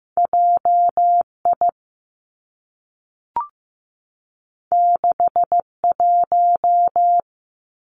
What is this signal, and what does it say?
Morse code, heard as a keyed tone, about 15 wpm, 700 Hz. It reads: J I 6 1
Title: Two Challenging Letter-Number Combinations 15wpm